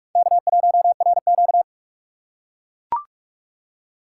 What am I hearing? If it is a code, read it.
Morse code, heard as a keyed tone, about 42 wpm, 700 Hz. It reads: K1RQ